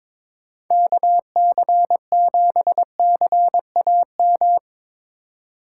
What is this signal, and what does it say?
Morse code, heard as a keyed tone, about 22 wpm, 700 Hz. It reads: KC7CAM
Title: Speed Racing - USA 2x3 Call Signs 22wpm